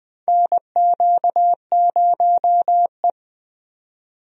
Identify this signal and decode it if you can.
Morse code, heard as a keyed tone, about 20 wpm, 700 Hz. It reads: NQ0E
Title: USA Call Signs - 4 Characters - 20wpm